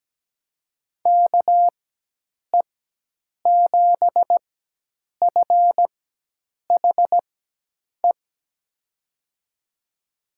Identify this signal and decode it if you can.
Morse code, heard as a keyed tone, about 17 wpm, spaced as for 8 wpm, 700 Hz. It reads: KE7FHE
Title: USA Call Signs - 6 Characters - 4x Character Spacing - 17wpm